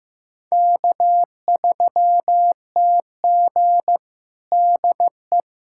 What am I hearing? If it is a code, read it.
Morse code, heard as a keyed tone, about 15 wpm, 700 Hz. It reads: K3TG DE
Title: USA Callsign DE Callsign - 4 Characters - 15wpm